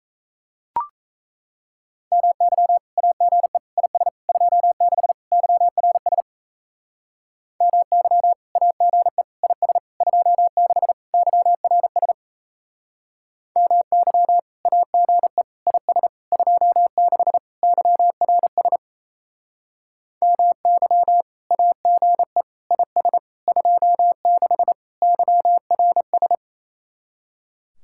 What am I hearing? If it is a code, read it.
Morse code, heard as a keyed tone, about 35 wpm, 700 Hz. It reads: MY AGE IS 26 YRS MY AGE IS 26 YRS MY AGE IS 26 YRS MY AGE IS 26 YRS
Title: Speed-Racing Common QSO Phrases - 28wpm